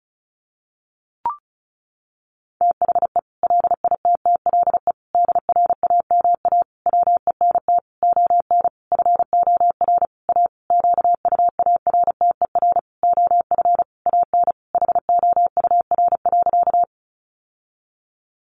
Morse code, heard as a keyed tone, about 34 wpm, 700 Hz. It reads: THE LITTLE DRAMA WENT ON FOR A QUARTER OF AN HOUR.
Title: Sentences from Top 1500 Words 35wpm